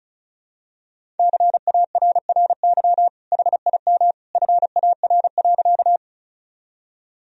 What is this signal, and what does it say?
Morse code, heard as a keyed tone, about 34 wpm, 700 Hz. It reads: CARRY HIM FAR.
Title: Sentences from Top 1000 Words - Review Part 1 - 35wpm